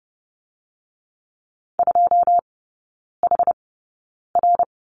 Morse code, heard as a keyed tone, about 30 wpm, spaced as for 9 wpm, 700 Hz. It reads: 2HR
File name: 081 Lesson 14e - Number 2 - Three Characters 30wpm